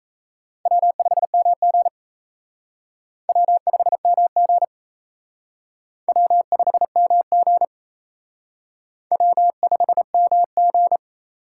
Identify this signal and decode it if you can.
Morse code, heard as a keyed tone, about 36 wpm, 700 Hz. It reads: W5MG W5MG W5MG W5MG